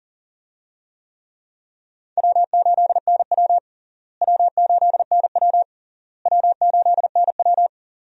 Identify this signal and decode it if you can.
Morse code, heard as a keyed tone, about 40 wpm, 700 Hz. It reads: W8NW W8NW W8NW
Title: USA Call Signs - 4 Characters - Repeated 3x - 40wpm